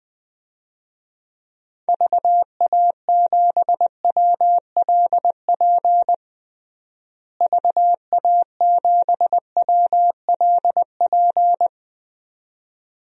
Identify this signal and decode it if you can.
Morse code, heard as a keyed tone, about 20 wpm, 700 Hz. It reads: VA7WLP VA7WLP
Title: DXCC-100 Call Signs - 6 Characters - Repeated 2x - 20wpm